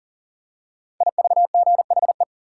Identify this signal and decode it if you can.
Morse code, heard as a keyed tone, about 38 wpm, 700 Hz. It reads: IVGHE
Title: Code Groups - 5 Letters - 40wpm